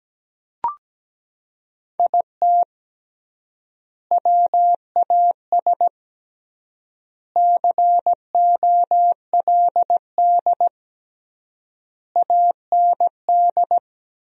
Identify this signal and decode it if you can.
Morse code, heard as a keyed tone, about 17 wpm, 700 Hz. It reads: IT WAS COLD AND
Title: 3x Word-Spacing - Sentences from Top 500 Words - 17wpm